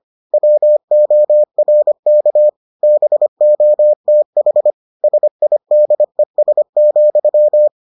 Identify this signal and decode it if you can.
Morse code, heard as a keyed tone, about 25 wpm, 600 Hz. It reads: WORK BOTH SIDES,